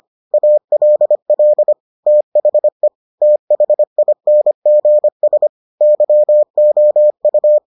Morse code, heard as a keyed tone, about 25 wpm, 600 Hz. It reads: ALL THE THINGS YOU